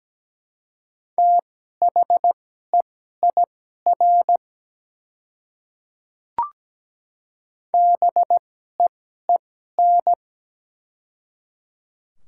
Morse code, heard as a keyed tone, about 17 wpm, spaced as for 12 wpm, 690 Hz. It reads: THEIR BEEN
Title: 2x Character Spacing - Top 100 Words - 17wpm